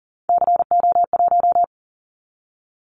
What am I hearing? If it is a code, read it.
Morse code, heard as a keyed tone, about 40 wpm, 700 Hz. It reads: CO1
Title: Callsign Prefixes for Top 100 DXCC - 40wpm